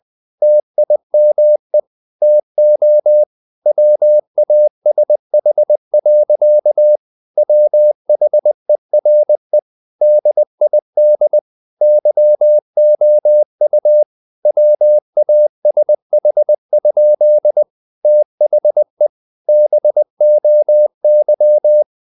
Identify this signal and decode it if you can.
Morse code, heard as a keyed tone, about 20 wpm, 600 Hz. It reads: TIME TO WASH. WHERE DID YOU WASH? THE BOY
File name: the old man and the sea / the old man and the sea part 1 20wpm